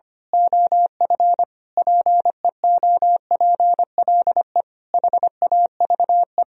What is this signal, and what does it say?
Morse code, heard as a keyed tone, about 24 wpm, 700 Hz. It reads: OF PEOPLE HAVE
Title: Speed-Racing - Sets of 5 Words 25wpm